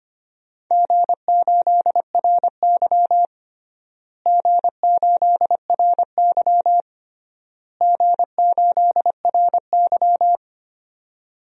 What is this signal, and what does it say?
Morse code, heard as a keyed tone, about 25 wpm, 700 Hz. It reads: G8RY G8RY G8RY
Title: DXCC-100 Call Signs - 4 Characters - Repeated 3x - 25wpm